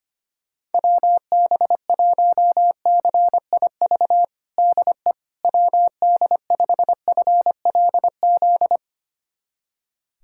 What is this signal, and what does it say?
Morse code, heard as a keyed tone, about 25 wpm, 700 Hz. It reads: WB1CIV DE WD5FLZ
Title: USA Callsign DE Callsign - 6 Characters - 25wpm